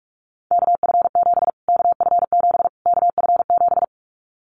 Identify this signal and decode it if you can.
Morse code, heard as a keyed tone, about 44 wpm, 700 Hz. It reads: KF7 KF7 KF7
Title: Callsign Prefixes for Top 25 DXCC - Repeated 3x - 45wpm